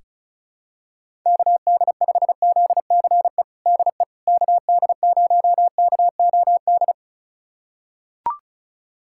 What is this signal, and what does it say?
Morse code, heard as a keyed tone, about 34 wpm, 700 Hz. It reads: KD5ZCE DE KD0KOD
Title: USA Callsign DE Callsign - 6 Characters - 35wpm